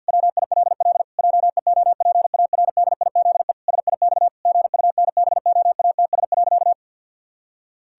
Morse code, heard as a keyed tone, about 50 wpm, 700 Hz. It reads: WILL JEOPARDIZE SIX GUNBOATS.
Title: Pangram Challenge v2 - 50wpm